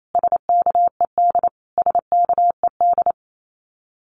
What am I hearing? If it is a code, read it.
Morse code, heard as a keyed tone, about 28 wpm, 700 Hz. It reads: SKED SKED
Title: Abbreviations Repeated 3x - 28wpm